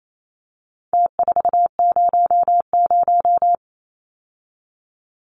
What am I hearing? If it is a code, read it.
Morse code, heard as a keyed tone, about 28 wpm, 700 Hz. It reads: T400
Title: Numbers - Contest Style - 4-Digit 28wpm